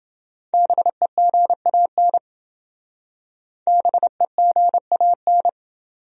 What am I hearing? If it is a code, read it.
Morse code, heard as a keyed tone, about 28 wpm, 700 Hz. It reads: BEGAN BEGAN